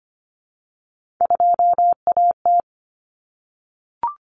Morse code, heard as a keyed tone, about 25 wpm, 700 Hz. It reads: 2AT
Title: Common Cut Numbers - 3-Digit 25wpm